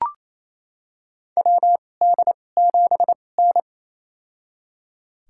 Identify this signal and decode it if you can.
Morse code, heard as a keyed tone, about 28 wpm, spaced as for 20 wpm, 700 Hz. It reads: WD7N
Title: USA Call Signs - 4 Characters - 2x Character Spacing - 28wpm